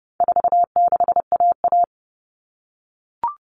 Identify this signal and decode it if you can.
Morse code, heard as a keyed tone, about 30 wpm, 700 Hz. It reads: 46AA
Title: All Cut Numbers - 4-Digit 30wpm